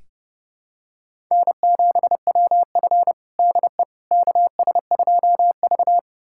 Morse code, heard as a keyed tone, about 32 wpm, 700 Hz. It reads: N7WF DE KS2V
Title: USA Callsign DE Callsign - 4 Characters - 30wpm